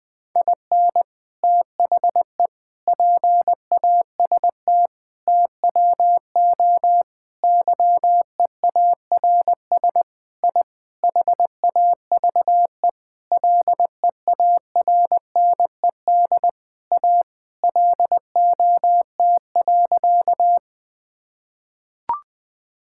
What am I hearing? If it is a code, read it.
Morse code, heard as a keyed tone, about 20 wpm, 700 Hz. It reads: IN THE PAST TWO YEARS I HAVE LEARNED A LOT.
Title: Sentences with 5-Word Phrases - 20wpm